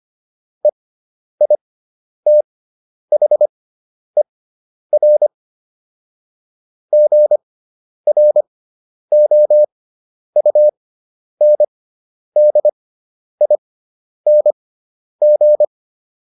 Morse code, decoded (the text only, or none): EITHER GROUNDING